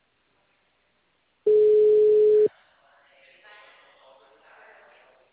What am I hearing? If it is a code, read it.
Ringback tone (Europe, 425 Hz), with speech.